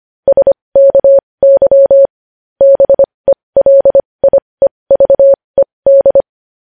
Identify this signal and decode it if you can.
Morse code, heard as a keyed tone, about 25 wpm, spaced as for 20 wpm, 550 Hz. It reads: SKY BELIEVED
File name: podcast / News headlines-1x-25x20